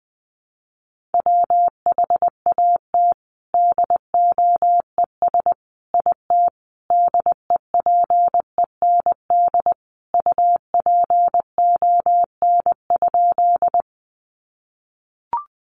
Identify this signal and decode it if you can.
Morse code, heard as a keyed tone, about 20 wpm, 700 Hz. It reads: WHAT DOES IT DEPEND UPON?
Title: Sentences from Top 1300 Words 20wpm